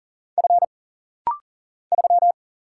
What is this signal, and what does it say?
Morse code, heard as a keyed tone, about 40 wpm, 700 Hz. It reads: F 3
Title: Single Letter-Number Alt - Rapid-Fire 40wpm